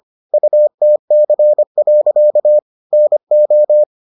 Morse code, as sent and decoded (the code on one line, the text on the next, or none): ..- - -.-. .-.-.- / -. ---
UTC. NO